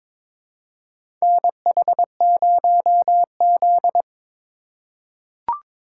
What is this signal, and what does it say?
Morse code, heard as a keyed tone, about 22 wpm, 700 Hz. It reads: NH0Z
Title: USA Call Signs - 4 Characters - 2x Character Spacing - 22wpm